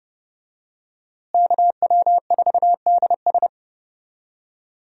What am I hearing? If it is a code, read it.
Morse code, heard as a keyed tone, about 30 wpm, 700 Hz. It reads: KW4DS
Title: USA Call Signs - 5 Characters - Repeated 2x - 30wpm